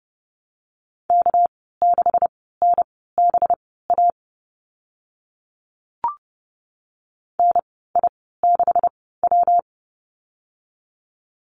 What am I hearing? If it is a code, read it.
Morse code, heard as a keyed tone, about 30 wpm, spaced as for 17 wpm, 700 Hz. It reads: K6NBA NI6W
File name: Lesson 37g - Letter x - Call Signs 30wpm